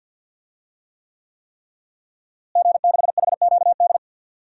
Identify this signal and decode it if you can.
Morse code, heard as a keyed tone, about 47 wpm, 700 Hz. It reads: G6HQD
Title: DXCC-100 Call Signs - 5 Characters - Repeated 3x - 50wpm